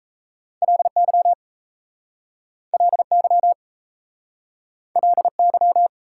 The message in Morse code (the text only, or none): LY LY LY